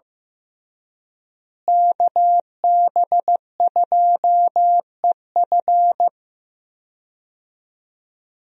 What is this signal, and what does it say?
Morse code, heard as a keyed tone, about 15 wpm, 700 Hz. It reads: KB2EF
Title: USA Call Signs - 5 Characters - Repeated 2x - 15wpm